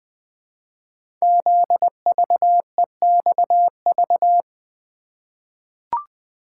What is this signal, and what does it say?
Morse code, heard as a keyed tone, about 20 wpm, 700 Hz. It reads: ZVEXV
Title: Code Groups - 5 Letters - 20wpm